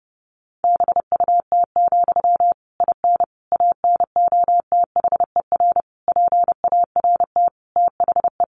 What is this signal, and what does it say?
Morse code, heard as a keyed tone, about 30 wpm, 700 Hz. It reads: BUT, IN ANOTHER PART THE
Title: Sentences from Top 1500 Words 30wpm